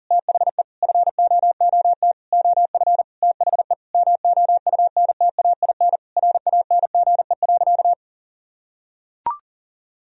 Morse code, heard as a keyed tone, about 40 wpm, 700 Hz. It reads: THE FOOT OF THE MOUNTAIN RANGE.
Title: Sentences from Top 1000 Words - Review Part 4 - 40wpm